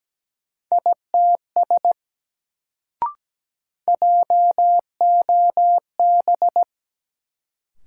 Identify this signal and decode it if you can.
Morse code, heard as a keyed tone, about 17 wpm, 700 Hz. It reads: ITS JOB
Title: Sentences from Top 600 Words - New Word Drill - 17wpm